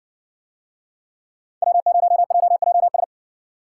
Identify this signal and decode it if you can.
Morse code, heard as a keyed tone, about 60 wpm, 700 Hz. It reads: W0JJS